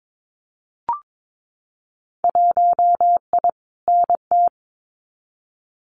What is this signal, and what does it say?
Morse code, heard as a keyed tone, about 22 wpm, 700 Hz. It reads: 1I NT
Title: Winter Field Day Exchanges Encore 22wpm